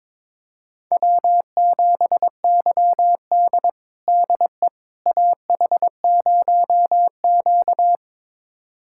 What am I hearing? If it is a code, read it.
Morse code, heard as a keyed tone, about 22 wpm, 700 Hz. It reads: W7YD DE AH0Q
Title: USA Callsign DE Callsign - 4 Characters - 22wpm